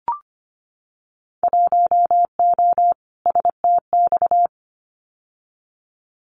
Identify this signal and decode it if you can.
Morse code, heard as a keyed tone, about 25 wpm, 700 Hz. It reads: 1O STX
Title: Winter Field Day Exchanges 25wpm